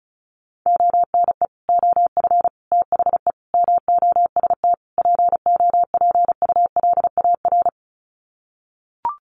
Morse code, heard as a keyed tone, about 34 wpm, 700 Hz. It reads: ONE OF THE MOST POPULAR
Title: Sets of 5 Words 35wpm